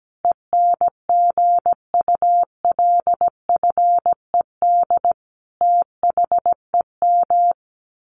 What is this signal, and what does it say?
Morse code, heard as a keyed tone, about 17 wpm, 700 Hz. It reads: ENGULFED THEM
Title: A Sound Of Thunder 17wpm